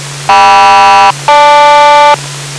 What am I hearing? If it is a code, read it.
Aircraft SELCAL, heard as two pairs of tones. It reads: KM-GL